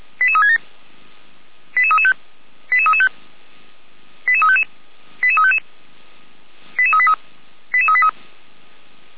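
Five-tone selective call (ZVEI), heard as quick runs of five tones, tones of about 80 ms. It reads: 88367 88335 88335 88355 88355 88383 88383
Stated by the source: ZVEI five-tone call received on SDR.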